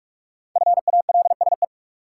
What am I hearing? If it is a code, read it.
Morse code, heard as a keyed tone, about 46 wpm, 700 Hz. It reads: FALSE
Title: Sentences from Top 1600 Words - New Word Drill - 45wpm